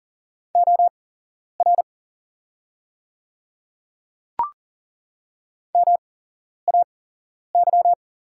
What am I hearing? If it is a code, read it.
Morse code, heard as a keyed tone, about 40 wpm, spaced as for 11 wpm, 700 Hz. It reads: OR MAY
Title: 8x Character Spacing - Top 100 Words - 40wpm